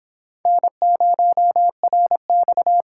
Morse code, heard as a keyed tone, about 29 wpm, 700 Hz. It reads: N0RX